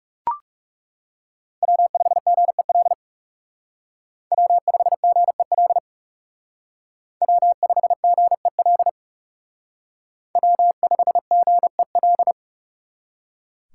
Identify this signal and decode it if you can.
Morse code, heard as a keyed tone, about 36 wpm, 700 Hz. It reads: W5GEL W5GEL W5GEL W5GEL
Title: Speed Racing - USA 1x3 Call Signs 30wpm